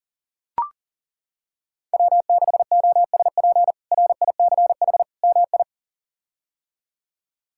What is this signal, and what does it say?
Morse code, heard as a keyed tone, about 40 wpm, 700 Hz. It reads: W6OSP RICH MI
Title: K1USN SST QSO Practice - 40wpm